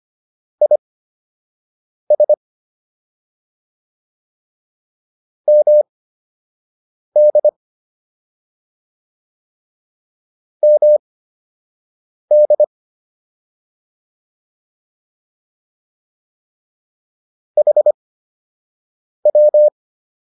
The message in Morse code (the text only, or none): IS MD MD HW